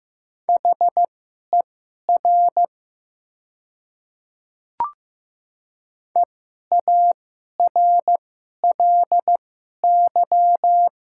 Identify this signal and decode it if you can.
Morse code, heard as a keyed tone, about 15 wpm, spaced as for 11 wpm, 700 Hz. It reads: HER EARLY
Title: 2x Character Spacing - Top 500 Words - 15wpm